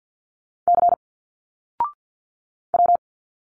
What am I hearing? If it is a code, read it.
Morse code, heard as a keyed tone, about 50 wpm, 700 Hz. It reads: C W
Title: Single Letters 50wpm